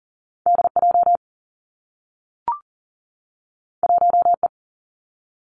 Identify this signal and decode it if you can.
Morse code, heard as a keyed tone, about 41 wpm, 700 Hz. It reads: DJ 1E
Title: Two Challenging Letter-Number Combinations 40wpm